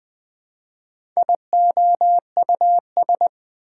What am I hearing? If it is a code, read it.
Morse code, heard as a keyed tone, about 21 wpm, 700 Hz. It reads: IOUS
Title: Speed-Racing Prefixes and Suffixes 17wpm